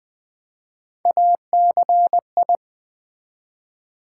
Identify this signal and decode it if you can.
Morse code, heard as a keyed tone, about 20 wpm, 700 Hz. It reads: ACI
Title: Build A Word - 1 Letter At A Time - 4-Letter Words - 20wpm